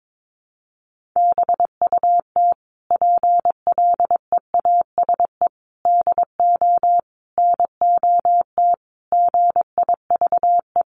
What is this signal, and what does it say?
Morse code, heard as a keyed tone, about 22 wpm, 700 Hz. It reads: BUT PLEASE DO NOT GIVE
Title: Sentences from Top 1500 Words 22wpm